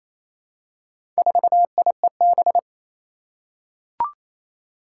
Morse code, heard as a keyed tone, about 29 wpm, 700 Hz. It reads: 4IEB
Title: Four Challenging Letter-Number Combinations 28wpm